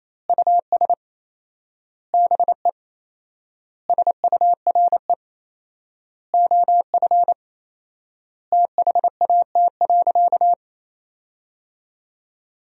Morse code, heard as a keyed tone, about 28 wpm, 700 Hz. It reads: US BE SURE OF THAT.
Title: 4x Word-Spacing - Sentences from Top 300 Words - 28wpm